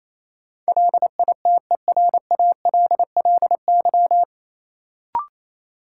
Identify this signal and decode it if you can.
Morse code, heard as a keyed tone, about 28 wpm, 700 Hz. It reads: LITERALLY
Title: Sentences from Top 1400 Words - New Word Drill - 28wpm